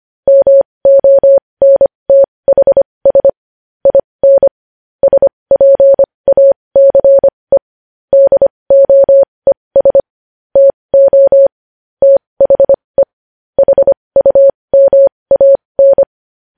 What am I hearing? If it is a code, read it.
Morse code, heard as a keyed tone, about 25 wpm, spaced as for 20 wpm, 550 Hz. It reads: MONTHS IN SPACE DOES TO THE HUMAN
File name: podcast / News headlines-1x-25x20